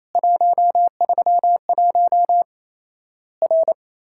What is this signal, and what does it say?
Morse code, heard as a keyed tone, about 28 wpm, 700 Hz. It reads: 131 R
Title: Worked All Europe DX Contest - QTC Exchange - 28wpm